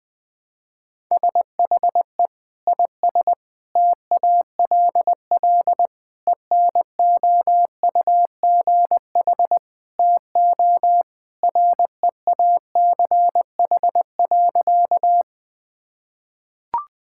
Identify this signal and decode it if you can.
Morse code, heard as a keyed tone, about 20 wpm, 700 Hz. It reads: SHE IS TALL ENOUGH TO REACH.
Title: Sentences with 2-Word Phrases - Spoken Beginning - 20wpm